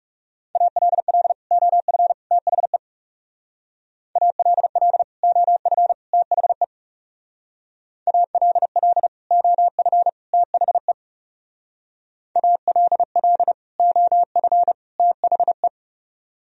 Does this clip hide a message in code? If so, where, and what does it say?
Yes: Morse code from 0.6 to 15.7 s, heard as a keyed tone, about 40 wpm, 700 Hz. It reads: ALL OF THE ALL OF THE ALL OF THE ALL OF THE